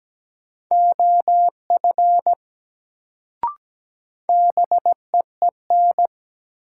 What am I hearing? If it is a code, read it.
Morse code, heard as a keyed tone, about 17 wpm, 700 Hz. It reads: OF BEEN
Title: Sentences from Top 100 Words - New Word Drill - 17wpm